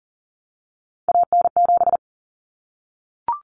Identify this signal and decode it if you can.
Morse code, heard as a keyed tone, about 40 wpm, 700 Hz. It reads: AN7
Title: Common Cut Numbers - 3-Digit 40wpm